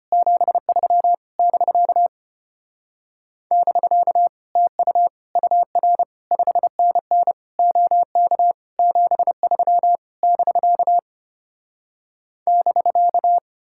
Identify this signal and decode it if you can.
Morse code, heard as a keyed tone, about 33 wpm, 700 Hz. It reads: 73 <BK> <BK> TU UR 5NN OK 73 <BK> <BK>